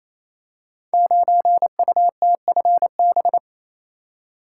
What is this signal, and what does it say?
Morse code, heard as a keyed tone, about 27 wpm, 700 Hz. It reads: 9UTFB
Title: Code Groups - 5 Letter:Numbers - 28wpm